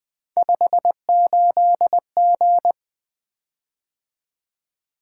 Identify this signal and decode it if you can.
Morse code, heard as a keyed tone, about 20 wpm, 700 Hz. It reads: 58G
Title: Three Letter-Number Combinations - 20wpm